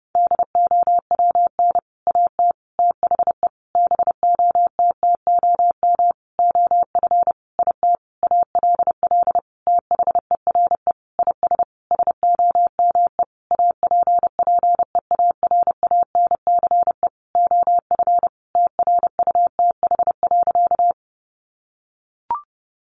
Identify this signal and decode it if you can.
Morse code, heard as a keyed tone, about 30 wpm, 700 Hz. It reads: DOWN AT THE BOTTOM OF IT ALL THERE IS SOME APPEARANCE OF TRUTH.
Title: Sentences from Top 1100 Words 30wpm